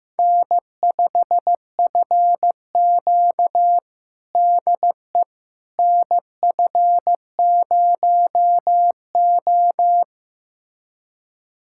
Morse code, heard as a keyed tone, about 16 wpm, 700 Hz. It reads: N5FQ DE NF0O
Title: USA Callsign DE Callsign - 4 Characters - 15wpm